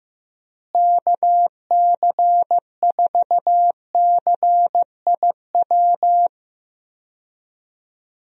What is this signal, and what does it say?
Morse code, heard as a keyed tone, about 16 wpm, 700 Hz. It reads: KC4CIW
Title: USA Call Signs - 6 Characters - Repeated 2x - 15wpm